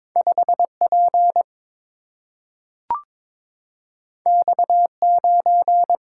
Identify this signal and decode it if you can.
Morse code, heard as a keyed tone, about 22 wpm, 700 Hz. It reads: 5P X9
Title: Single-Letter Single-Number 22wpm